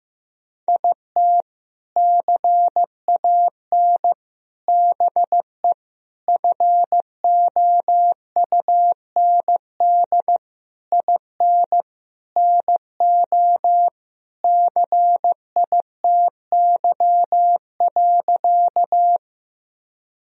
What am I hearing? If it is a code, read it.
Morse code, heard as a keyed tone, about 16 wpm, 700 Hz. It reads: IT CAN BE FOUND IN NO CITY.
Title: Sentences from Top 1000 Words - Review Part 1 - 15wpm